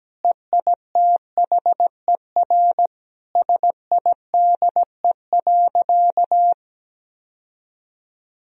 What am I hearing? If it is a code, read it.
Morse code, heard as a keyed tone, about 17 wpm, 700 Hz. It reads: EITHER SIDE.
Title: Sentences from Top 1000 Words - Review Part 4 - 17wpm